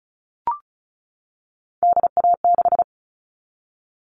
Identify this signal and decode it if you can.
Morse code, heard as a keyed tone, about 34 wpm, 700 Hz. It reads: DA6
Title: All Cut Numbers - 3-Digit 35wpm